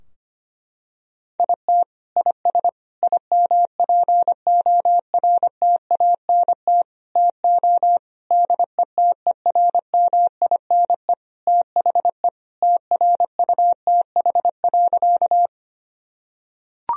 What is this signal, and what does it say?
Morse code, heard as a keyed tone, about 25 wpm, 700 Hz. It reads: IT IS IMPORTANT TO DETERMINE THE TRUTH.
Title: Sentences from Top 1000 Words - Review Part 4 - 25wpm